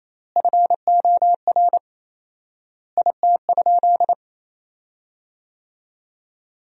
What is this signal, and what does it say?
Morse code, heard as a keyed tone, about 28 wpm, 700 Hz. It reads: FOR IT?
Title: 4x Word-Spacing - Sentences from Top 400 Words - 28wpm